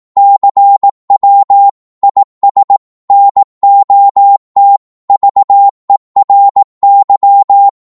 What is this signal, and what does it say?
Morse code, heard as a keyed tone, about 18 wpm, 800 Hz. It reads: CW IS NOT VERY
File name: Morse / qso-18wpm